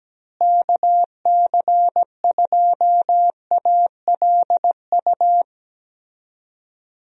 Morse code, heard as a keyed tone, about 17 wpm, 700 Hz. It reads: KC2ALU
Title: USA Call Signs - 6 Characters - Repeated 2x - 17wpm